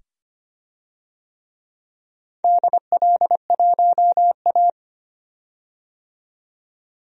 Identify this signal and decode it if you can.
Morse code, heard as a keyed tone, about 25 wpm, 700 Hz. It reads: DL1A
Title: DXCC-100 Call Signs - 4 Characters - 25wpm